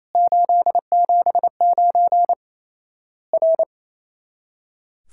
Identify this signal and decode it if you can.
Morse code, heard as a keyed tone, about 27 wpm, 700 Hz. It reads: 879 R
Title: Worked All Europe DX Contest - QTC Exchange - 28wpm